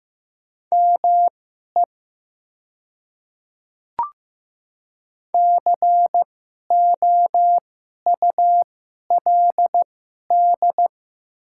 Morse code, heard as a keyed tone, about 15 wpm, spaced as for 11 wpm, 700 Hz. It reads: ME COULD
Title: 2x Character Spacing - Top 200 Words - 15wpm